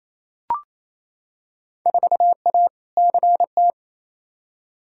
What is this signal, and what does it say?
Morse code, heard as a keyed tone, about 28 wpm, 700 Hz. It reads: 4A CT
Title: Field Day Exchanges Encore - 28wpm